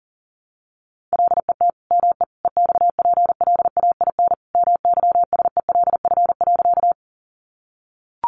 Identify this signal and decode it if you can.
Morse code, heard as a keyed tone, about 40 wpm, 700 Hz. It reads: LET ME EXPLAIN MYSELF.
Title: Sentences from Top 1300 Words 40wpm